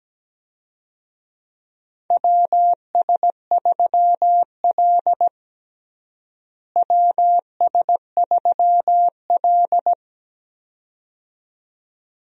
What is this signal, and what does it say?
Morse code, heard as a keyed tone, about 17 wpm, 700 Hz. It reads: WS3L WS3L
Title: USA Call Signs - 4 Characters - Repeated 2x - 17wpm